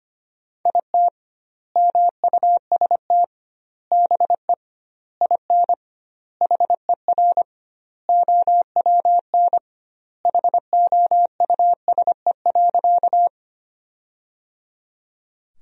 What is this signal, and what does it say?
Morse code, heard as a keyed tone, about 25 wpm, 700 Hz. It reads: IT MUST BE IN HER OWN HOUSE.
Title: 2x Word-Spacing - Sentences from Top 300 Words - 25wpm